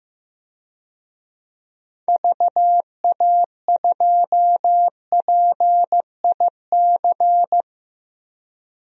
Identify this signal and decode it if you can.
Morse code, heard as a keyed tone, about 16 wpm, 700 Hz. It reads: VA2PIC
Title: DXCC-100 Call Signs - 6 Characters - Repeated 3x - 15wpm